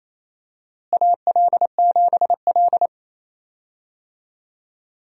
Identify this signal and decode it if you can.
Morse code, heard as a keyed tone, about 28 wpm, 700 Hz. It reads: AL7L
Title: USA Call Signs - 4 Characters - Repeated 3x - 28wpm